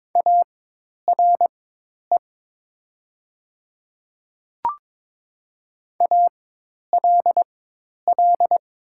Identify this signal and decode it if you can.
Morse code, heard as a keyed tone, about 22 wpm, spaced as for 10 wpm, 700 Hz. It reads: ARE ALL
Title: 4x Character Spacing - Top 100 Words - 22wpm